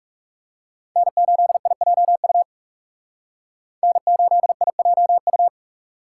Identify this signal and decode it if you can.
Morse code, heard as a keyed tone, about 38 wpm, 700 Hz. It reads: N8IJU N8IJU